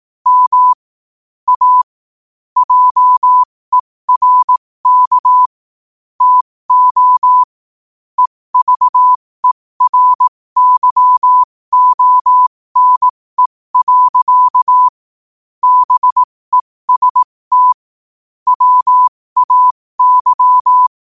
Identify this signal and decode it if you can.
Morse code, heard as a keyed tone, about 18 wpm, spaced as for 15 wpm, 1000 Hz. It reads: M A JERK TO EVERYONE. BEST WAY
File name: qotdfiles / 15wpm